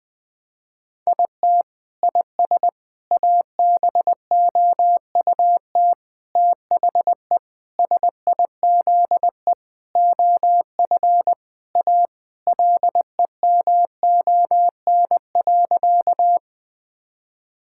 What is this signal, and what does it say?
Morse code, heard as a keyed tone, about 20 wpm, 700 Hz. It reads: IT IS ABOUT THE SIZE OF A LEMON.
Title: Sentences with 5-Word Phrases - 20wpm